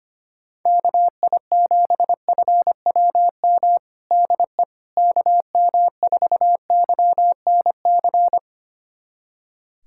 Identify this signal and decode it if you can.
Morse code, heard as a keyed tone, about 25 wpm, 700 Hz. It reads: KI7FWM DE KM4YNC
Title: USA Callsign DE Callsign - 6 Characters - 25wpm